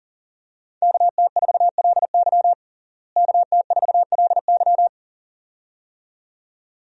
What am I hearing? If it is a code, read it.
Morse code, heard as a keyed tone, about 40 wpm, 700 Hz. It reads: KT4LY KT4LY
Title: USA Call Signs - 5 Characters - Repeated 2x - 40wpm